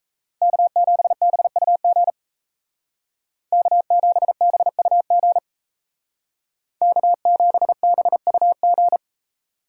Morse code, heard as a keyed tone, about 35 wpm, 700 Hz. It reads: K7BUG K7BUG K7BUG
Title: Speed Racing - USA 1x3 Call Signs 28wpm